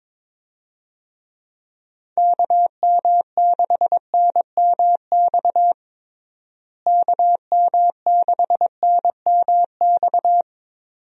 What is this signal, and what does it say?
Morse code, heard as a keyed tone, about 22 wpm, 700 Hz. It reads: KM6NMX KM6NMX
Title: USA Call Signs - 6 Characters - Repeated 2x - 22wpm